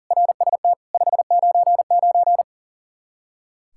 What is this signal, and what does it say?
Morse code, heard as a keyed tone, about 40 wpm, 700 Hz. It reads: RST 599
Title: Practice QSO - Exchange 1 - Beginning - 40wpm